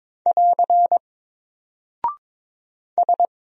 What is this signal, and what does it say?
Morse code, heard as a keyed tone, about 22 wpm, 700 Hz. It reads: <AR> S
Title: Single Character (Letter:Number:Punctuation:Prosign) 22wpm